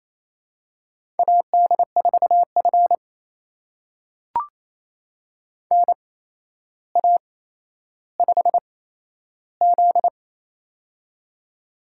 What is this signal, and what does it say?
Morse code, heard as a keyed tone, about 27 wpm, spaced as for 12 wpm, 700 Hz. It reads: AD4F NA5Z
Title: USA Call Signs - 4 Characters - 8x Character Spacing - 28wpm